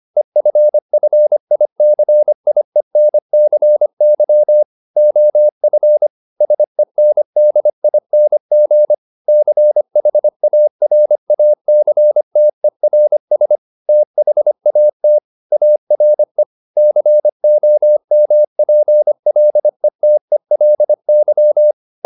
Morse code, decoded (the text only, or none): EFFICIENCY OF SENDING CHARACTERS THAT ARE COMPLETELY